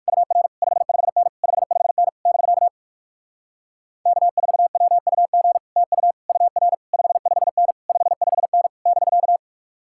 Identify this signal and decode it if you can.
Morse code, heard as a keyed tone, about 59 wpm, 700 Hz. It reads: UR 55N 55N <BK> K4WUG TU UR 55N 55N <BK>